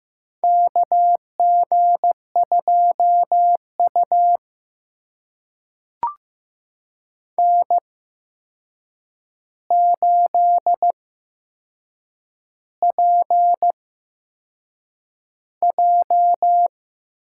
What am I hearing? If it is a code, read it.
Morse code, heard as a keyed tone, about 15 wpm, spaced as for 6 wpm, 700 Hz. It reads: KG2U N8PJ
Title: USA Call Signs - 4 Characters - 8x Character Spacing - 15wpm